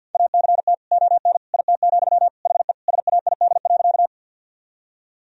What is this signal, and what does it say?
Morse code, heard as a keyed tone, about 51 wpm, 700 Hz. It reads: ACT ON IT, HE SAID.
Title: Sentences from Top 1000 Words - Review Part 1 - 50wpm